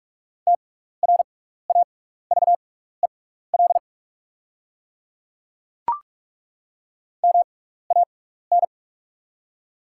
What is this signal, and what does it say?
Morse code, heard as a keyed tone, about 43 wpm, spaced as for 15 wpm, 700 Hz. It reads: TRAVEL MAN
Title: 6x Character Spacing - Top 500 Words - 45wpm